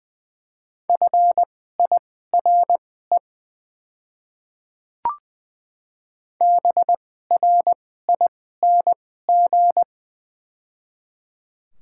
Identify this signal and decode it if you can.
Morse code, heard as a keyed tone, about 20 wpm, spaced as for 14 wpm, 700 Hz. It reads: FIRE BRING
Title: 2x Character Spacing - Top 500 Words - 20wpm